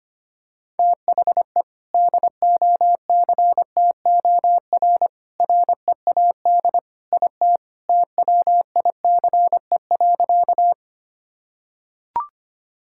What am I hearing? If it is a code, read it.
Morse code, heard as a keyed tone, about 25 wpm, 700 Hz. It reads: THE DOCTOR READ IT TWICE.
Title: Sentences from Top 1600 Words 25wpm